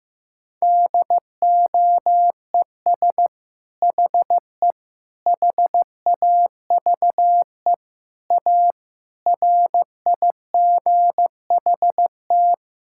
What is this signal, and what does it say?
Morse code, heard as a keyed tone, about 16 wpm, 700 Hz. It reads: DOES HE HAVE A RIGHT
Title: Sentences from Top 1000 Words - Review Part 5 - 15wpm